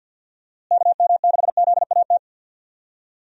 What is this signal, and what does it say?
Morse code, heard as a keyed tone, about 50 wpm, 700 Hz. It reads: KM6ZAT
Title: USA Call Signs - 6 Characters - Repeated 2x - 50wpm